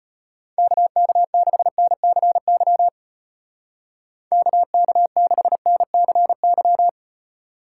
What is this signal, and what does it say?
Morse code, heard as a keyed tone, about 30 wpm, 700 Hz. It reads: KK6NCY KK6NCY
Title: Speed Racing - USA 2x3 Call Signs 25wpm